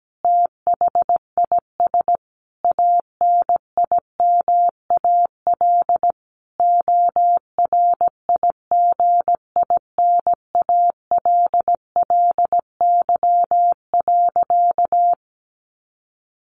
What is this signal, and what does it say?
Morse code, heard as a keyed tone, about 17 wpm, 700 Hz. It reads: THIS ANIMAL ORIGINALLY.
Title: A Sound Of Thunder 17wpm